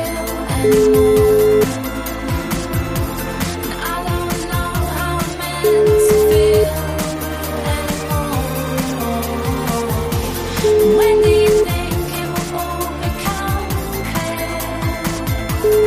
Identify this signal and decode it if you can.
Ringback tone (Europe, 425 Hz), with noise.